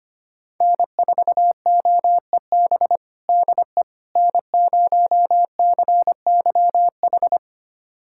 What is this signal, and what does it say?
Morse code, heard as a keyed tone, about 25 wpm, 700 Hz. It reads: N4OEB DE N0CYH
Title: USA Callsign DE Callsign 25wpm